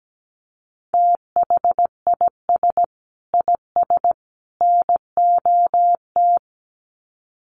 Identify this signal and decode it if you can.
Morse code, heard as a keyed tone, about 17 wpm, 700 Hz. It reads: THIS IS NOT
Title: Sets of 3 Words 17wpm